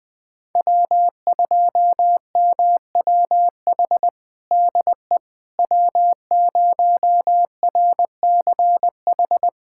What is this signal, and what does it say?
Morse code, heard as a keyed tone, about 20 wpm, 700 Hz. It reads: W2MWH DE W0RCH
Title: USA Callsign DE Callsign - 5 Characters - 20wpm